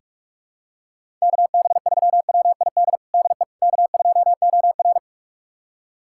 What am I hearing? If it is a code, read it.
Morse code, heard as a keyed tone, about 47 wpm, 700 Hz. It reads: KB3WID DE K2OR
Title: USA Callsign DE Callsign 45wpm